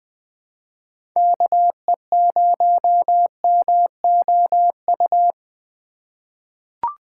Morse code, heard as a keyed tone, about 20 wpm, 700 Hz. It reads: KE0MOU
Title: USA Call Signs - 2x Character Spacing - 20wpm